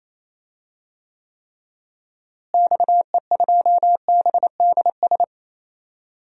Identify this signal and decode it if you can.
Morse code, heard as a keyed tone, about 28 wpm, 700 Hz. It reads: XE2BDS
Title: DXCC-100 Call Signs - 6 Characters - Repeated 3x - 28wpm